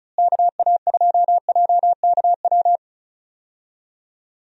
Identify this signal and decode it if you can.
Morse code, heard as a keyed tone, about 36 wpm, 700 Hz. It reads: KA2JKW
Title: USA Call Signs - 6 Characters - Repeated 3x - 35wpm